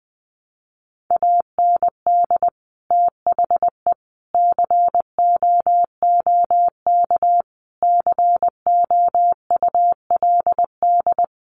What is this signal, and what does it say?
Morse code, heard as a keyed tone, about 20 wpm, 700 Hz. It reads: AND THE COOK COULD
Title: Sentences from Top 1300 Words 20wpm